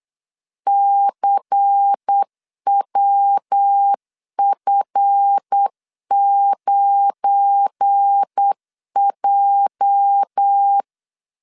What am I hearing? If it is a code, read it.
Morse code, heard as a keyed tone, about 8 wpm, 800 Hz. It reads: CWF9J
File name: LEZIONI / 9 2 8wpm 3437